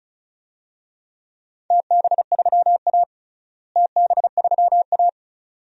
Morse code, heard as a keyed tone, about 34 wpm, 700 Hz. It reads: TB3A TB3A
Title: DXCC-100 Call Signs - 4 Characters - Repeated 2x - 35wpm